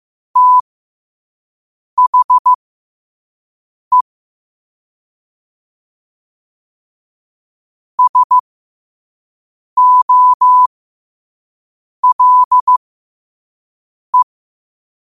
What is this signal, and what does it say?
Morse code, heard as a keyed tone, about 15 wpm, spaced as for 5 wpm, 1000 Hz. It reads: THE SOLE
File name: qotdfiles / 5wpm